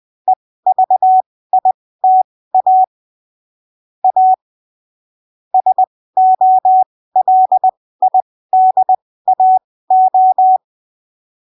Morse code, heard as a keyed tone, about 20 wpm, spaced as for 15 wpm, 750 Hz. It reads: EVITA A SOLIDAO
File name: arquivos / texto / texto 23 20ppm